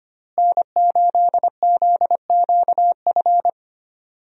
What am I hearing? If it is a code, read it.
Morse code, heard as a keyed tone, about 25 wpm, 700 Hz. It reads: N8ZQF
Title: USA Call Signs - 5 Characters - Repeated 3x - 25wpm